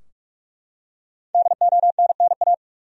Morse code, heard as a keyed tone, about 41 wpm, 700 Hz. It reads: DONNA
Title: USA Common Names - 45wpm